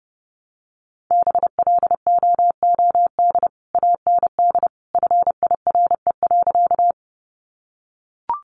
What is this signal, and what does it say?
Morse code, heard as a keyed tone, about 30 wpm, 700 Hz. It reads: BLOOD AND FIRE.
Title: Sentences from Top 1100 Words 30wpm